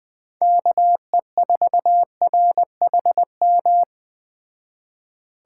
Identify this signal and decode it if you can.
Morse code, heard as a keyed tone, about 20 wpm, 700 Hz. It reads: KE4RHM